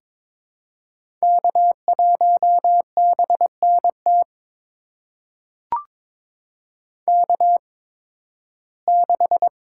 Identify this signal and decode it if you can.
Morse code, heard as a keyed tone, about 22 wpm, spaced as for 14 wpm, 700 Hz. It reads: K1BNT K6
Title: USA Call Signs - 5 Characters - 8x Character Spacing - 22wpm